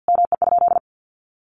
Morse code, heard as a keyed tone, about 50 wpm, 700 Hz. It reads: ME?
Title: Sentences from Top 1400 Words 50wpm